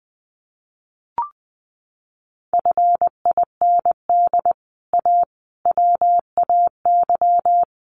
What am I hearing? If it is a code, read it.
Morse code, heard as a keyed tone, about 20 wpm, 700 Hz. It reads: FIND A WAY
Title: Sets of 4 Words 20wpm